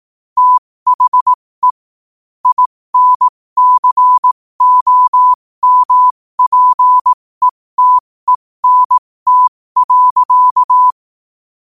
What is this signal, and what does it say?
Morse code, heard as a keyed tone, about 18 wpm, spaced as for 15 wpm, 1000 Hz. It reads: THE INCOMPETENT.
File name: qotdfiles / 15wpm